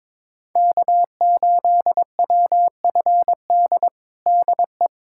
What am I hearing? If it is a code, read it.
Morse code, heard as a keyed tone, about 22 wpm, 700 Hz. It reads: K8WFD DE
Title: USA Callsign DE Callsign - 5 Characters - 22wpm